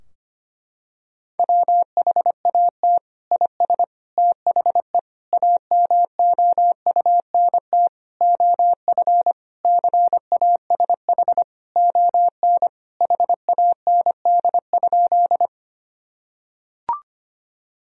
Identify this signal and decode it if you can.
Morse code, heard as a keyed tone, about 25 wpm, 700 Hz. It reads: WHAT IS THE AMOUNT OF CASH ON HAND?
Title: Sentences from Top 1000 Words - Review Part 4 - 25wpm